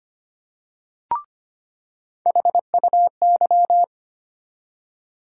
Morse code, heard as a keyed tone, about 25 wpm, 700 Hz. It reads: HUY
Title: USA Call Sign Postfixes - 3 Letters 25wpm